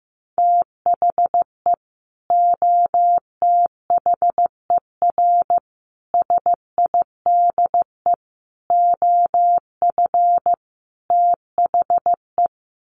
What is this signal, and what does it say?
Morse code, heard as a keyed tone, about 15 wpm, 700 Hz. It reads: THE OTHER SIDE OF THE
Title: Sets of 5 Words 15wpm